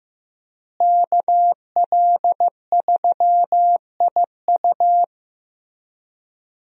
Morse code, heard as a keyed tone, about 15 wpm, 700 Hz. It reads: KL3IU
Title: USA Call Signs - 5 Characters - Repeated 3x - 15wpm